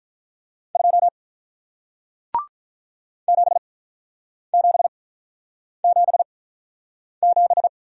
Morse code, heard as a keyed tone, about 46 wpm, spaced as for 9 wpm, 700 Hz. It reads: 2 7777